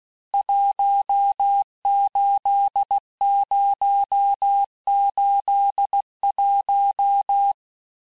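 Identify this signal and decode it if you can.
Morse code, heard as a keyed tone, about 16 wpm, 800 Hz. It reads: 18081
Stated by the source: Morse course text, end of QSO, sent at 15 words per minute.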